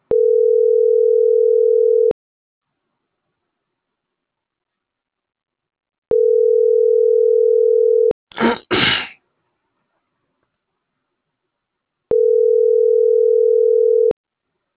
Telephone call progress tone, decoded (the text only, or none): ringback tone